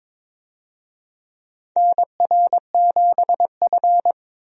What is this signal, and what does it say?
Morse code, heard as a keyed tone, about 22 wpm, 700 Hz. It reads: NR7F
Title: USA Call Signs - 4 Characters - Repeated 2x - 22wpm